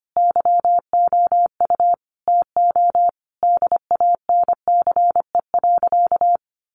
Morse code, heard as a keyed tone, about 25 wpm, 700 Hz. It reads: YOU TO DANCE.
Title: Sentences from Top 700 Words 25wpm